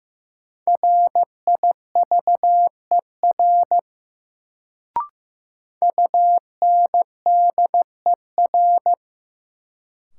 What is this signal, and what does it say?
Morse code, heard as a keyed tone, about 15 wpm, 700 Hz. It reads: RIVER UNDER
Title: Sentences from Top 300 Words - New Word Drill - 15wpm